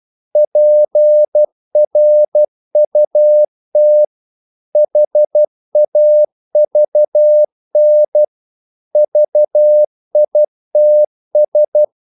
Morse code, heard as a keyed tone, about 12 wpm, 600 Hz. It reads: PRUT HAVN VITS
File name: Korte ord DK 12wpm